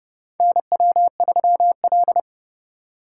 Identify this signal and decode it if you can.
Morse code, heard as a keyed tone, about 29 wpm, 700 Hz. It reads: NW3L